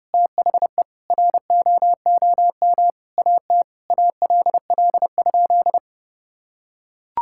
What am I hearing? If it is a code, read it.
Morse code, heard as a keyed tone, about 29 wpm, 700 Hz. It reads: THE ROOM AT ALL?
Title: Sentences from Top 1600 Words 30wpm